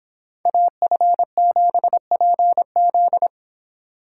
Morse code, heard as a keyed tone, about 26 wpm, 700 Hz. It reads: AF7PZ